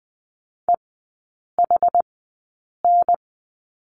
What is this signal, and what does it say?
Morse code, heard as a keyed tone, about 20 wpm, spaced as for 8 wpm, 700 Hz. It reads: EHN
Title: 127 Lesson 21e - Number 6 - Three Characters 20wpm